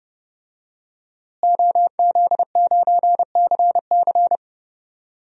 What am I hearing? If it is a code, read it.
Morse code, heard as a keyed tone, about 30 wpm, 700 Hz. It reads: OZ9CC
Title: DXCC-100 Call Signs - 5 Characters - Repeated 2x - 30wpm